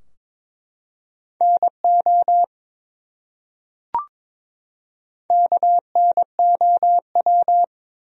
Morse code, heard as a keyed tone, about 22 wpm, 700 Hz. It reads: NO KNOW
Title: Top 100 Words 22wpm